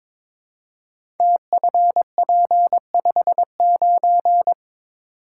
Morse code, heard as a keyed tone, about 23 wpm, 700 Hz. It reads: TFP59
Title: Code Groups - 5 Letter:Numbers - 22wpm